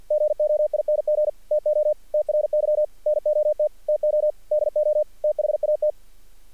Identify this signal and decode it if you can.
Morse code, heard as a keyed tone, about 48 wpm, 600 Hz. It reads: GOING TO TRY NOT TO DO THAT
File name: archived / CRSnet